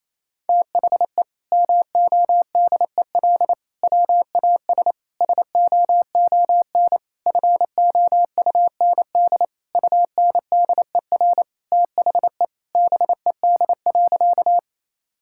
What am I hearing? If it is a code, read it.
Morse code, heard as a keyed tone, about 28 wpm, 700 Hz. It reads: THE MODEL WAS SOON FOUND UNDER THE BED.
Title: Sentences from Top 1000 Words - Review Part 5 - 28wpm